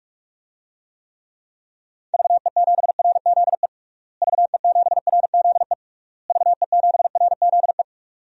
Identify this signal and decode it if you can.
Morse code, heard as a keyed tone, about 45 wpm, 700 Hz. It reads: VE7RZE VE7RZE VE7RZE
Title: DXCC-100 Call Signs - 6 Characters - Repeated 3x - 45wpm